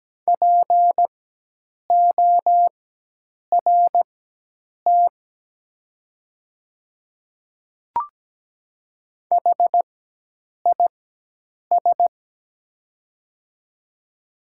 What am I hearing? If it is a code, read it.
Morse code, heard as a keyed tone, about 17 wpm, spaced as for 8 wpm, 700 Hz. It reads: PORT HIS
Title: 4x Character Spacing - Top 200 Words - 17wpm